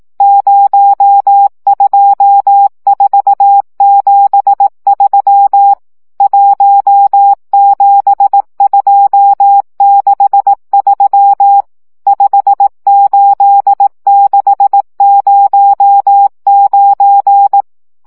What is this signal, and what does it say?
Morse code, heard as a keyed tone, about 18 wpm, 800 Hz. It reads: 02473 17263 58609
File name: morze vizsga 2011-3 15